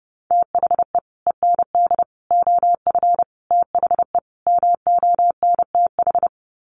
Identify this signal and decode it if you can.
Morse code, heard as a keyed tone, about 30 wpm, 700 Hz. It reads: THE END OF THE MONTH
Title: Sets of 5 Words Encore 30wpm